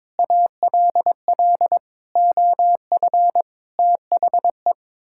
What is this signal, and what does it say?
Morse code, heard as a keyed tone, about 22 wpm, 700 Hz. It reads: ALL OF THE
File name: practice / 22wpm / Sets of 6 Words - 22wpm